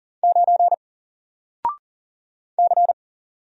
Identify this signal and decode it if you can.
Morse code, heard as a keyed tone, about 40 wpm, 700 Hz. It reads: 9 C
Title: Single Character (Letter:Number:Punctuation:Prosign) Alt 40wpm